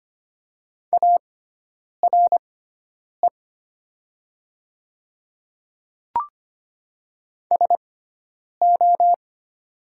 Morse code, heard as a keyed tone, about 25 wpm, spaced as for 9 wpm, 690 Hz. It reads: ARE SO
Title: 6x Character Spacing - Top 200 Words - 25wpm